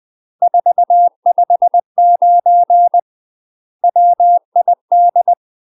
Morse code, heard as a keyed tone, about 20 wpm, 700 Hz. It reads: 459 WID
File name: UK MASTER QSOs TRAINING / 20WPM / UK MASTER QSOs 0004